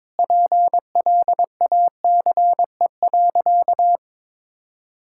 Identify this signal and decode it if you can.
Morse code, heard as a keyed tone, about 22 wpm, 700 Hz. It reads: PLACE.
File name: practice / 22wpm / Sentences with 8-Word Phrases - 22wpm